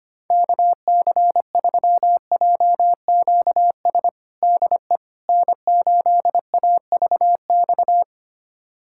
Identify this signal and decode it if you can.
Morse code, heard as a keyed tone, about 25 wpm, 700 Hz. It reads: KC3JQS DE N8AVX
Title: USA Callsign DE Callsign 25wpm